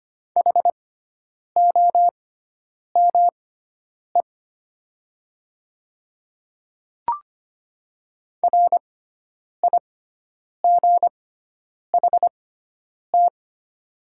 Morse code, heard as a keyed tone, about 25 wpm, spaced as for 9 wpm, 700 Hz. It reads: HOME RIGHT
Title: 6x Character Spacing - Top 200 Words - 25wpm